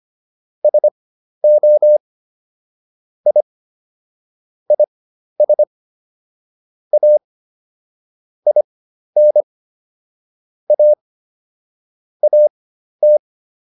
Morse code, heard as a keyed tone, about 25 wpm, spaced as for 12 wpm, 600 Hz. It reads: SO I IS A IN A AT